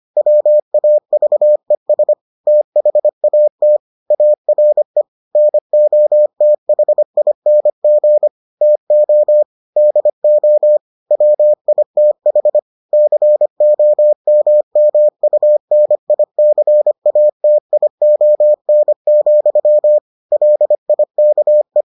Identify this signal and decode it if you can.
Morse code, heard as a keyed tone, about 25 wpm, 600 Hz. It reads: WAVES THAT ARE NOTHING TO DO WITH COMMUNICATION, LIKE